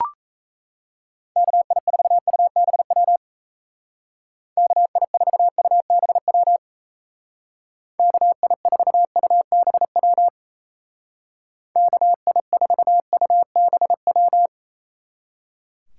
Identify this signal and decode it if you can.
Morse code, heard as a keyed tone, about 35 wpm, 700 Hz. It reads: KI4UBW KI4UBW KI4UBW KI4UBW
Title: Speed Racing - USA 2x3 Call Signs 28wpm